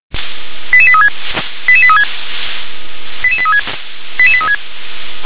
Five-tone selective call (ZVEI), heard as quick runs of five tones, tones of about 70 ms. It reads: 88836 88836 88836 88836